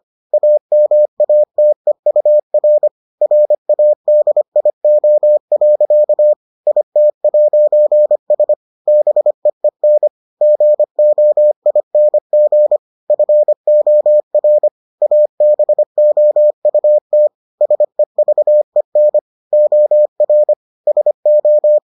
Morse code, heard as a keyed tone, about 25 wpm, 600 Hz. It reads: AMATEUR RADIO. IT'S BEEN GOING FOR ABOUT SEVEN OR SO